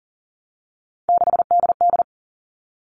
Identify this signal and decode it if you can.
Morse code, heard as a keyed tone, about 40 wpm, 700 Hz. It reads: 6DD
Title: All Cut Numbers - 3-Digit 40wpm